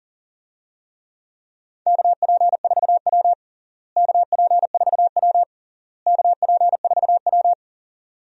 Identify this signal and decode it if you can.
Morse code, heard as a keyed tone, about 37 wpm, 700 Hz. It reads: KP4W KP4W KP4W
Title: DXCC-100 Call Signs - 4 Characters - Repeated 3x - 40wpm